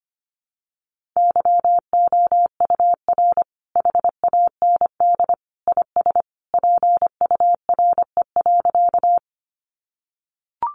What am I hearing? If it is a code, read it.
Morse code, heard as a keyed tone, about 25 wpm, 700 Hz. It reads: YOUR HAND IS PURE.
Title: Sentences from Top 1500 Words 25wpm